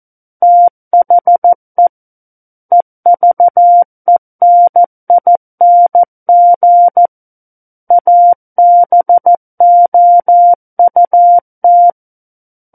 Morse code, heard as a keyed tone, about 14 wpm, 700 Hz. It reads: THE EVENING ABOUT
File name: Book One / 05-Chapter 5 - 14 WPM